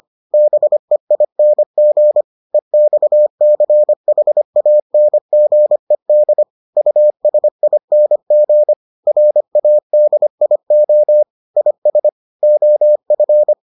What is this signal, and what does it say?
Morse code, heard as a keyed tone, about 25 wpm, 600 Hz. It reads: BEING EXCHANGED USING RADIO IS OF